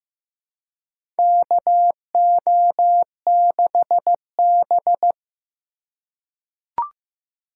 Morse code, heard as a keyed tone, about 16 wpm, 700 Hz. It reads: KO6B
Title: USA Call Signs - 2x Character Spacing - 15wpm